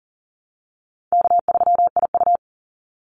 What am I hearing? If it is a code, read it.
Morse code, heard as a keyed tone, about 40 wpm, 700 Hz. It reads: K3IU
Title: Easy Call Signs Repeated 3x - 40wpm